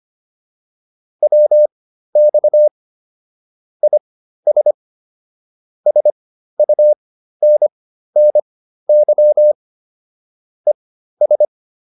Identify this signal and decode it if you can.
Morse code, heard as a keyed tone, about 25 wpm, spaced as for 13 wpm, 600 Hz. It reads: WX IS SUNNY ES
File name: QSO 305 13